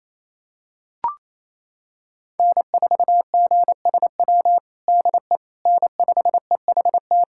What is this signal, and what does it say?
Morse code, heard as a keyed tone, about 28 wpm, 700 Hz. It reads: N4GSW DE N5EHT
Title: USA Callsign DE Callsign - 5 Characters - 28wpm